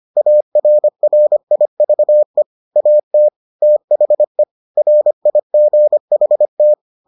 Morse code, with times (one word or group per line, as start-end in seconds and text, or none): 0.2-2.4 ARRIVE
2.8-3.3 AT
3.6-4.4 THE
4.8-6.7 RIGHT